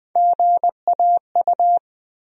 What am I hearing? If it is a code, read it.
Morse code, heard as a keyed tone, about 20 wpm, 700 Hz. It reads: GAU
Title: USA Call Sign Postfixes - 3 Letters 20wpm